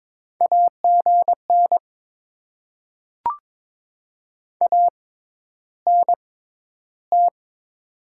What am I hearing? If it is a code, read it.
Morse code, heard as a keyed tone, about 22 wpm, spaced as for 11 wpm, 700 Hz. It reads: AGN ANT
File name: Top 20 QSO Elements - 6x Character Spacing - 22wpm